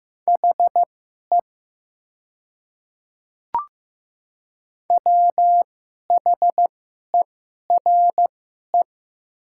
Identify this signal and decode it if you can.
Morse code, heard as a keyed tone, about 15 wpm, spaced as for 11 wpm, 700 Hz. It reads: HE WHERE
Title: 2x Character Spacing - Top 300 Words - 15wpm